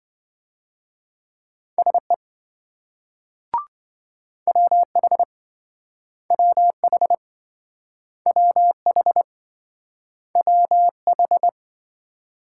Morse code, heard as a keyed tone, about 26 wpm, 700 Hz. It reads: SE WH WH WH WH